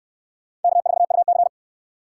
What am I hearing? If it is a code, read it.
Morse code, heard as a keyed tone, about 61 wpm, 700 Hz. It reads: K4UC